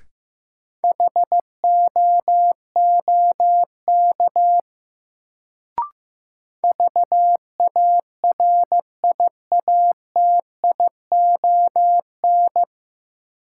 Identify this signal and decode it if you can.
Morse code, heard as a keyed tone, about 15 wpm, 700 Hz. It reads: HOOK VARIATION
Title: Sentences from Top 1600 Words - New Word Drill - 15wpm